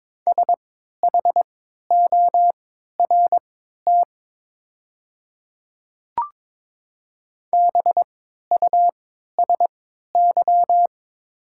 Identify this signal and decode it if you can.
Morse code, heard as a keyed tone, about 22 wpm, spaced as for 13 wpm, 700 Hz. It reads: SHORT BUSY
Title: 3x Character Spacing - Top 500 Words - 22wpm